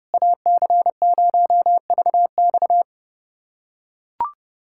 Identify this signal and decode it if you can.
Morse code, heard as a keyed tone, about 32 wpm, 700 Hz. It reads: AC0VX
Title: USA Callsign DE Callsign - 5 Characters - 30wpm